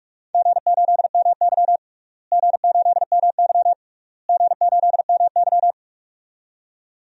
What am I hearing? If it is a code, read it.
Morse code, heard as a keyed tone, about 46 wpm, 690 Hz. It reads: G8MY G8MY G8MY
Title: DXCC-100 Call Signs - 4 Characters - Repeated 3x - 45wpm